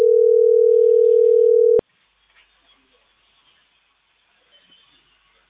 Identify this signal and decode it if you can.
Ringback tone (North America), 440 + 480 Hz.